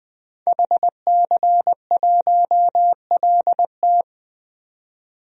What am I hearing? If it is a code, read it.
Morse code, heard as a keyed tone, about 20 wpm, 700 Hz. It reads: HC1LT
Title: DXCC-100 Call Signs - 5 Characters - 20wpm